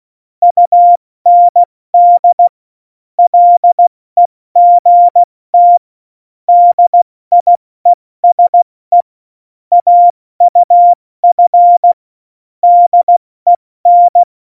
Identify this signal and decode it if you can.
Morse code, heard as a keyed tone, about 16 wpm, 700 Hz. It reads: UND LEGT DIESE AUF DEN
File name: Morsekurs / Witze / Witz 15. Oktober 2019, 70 F80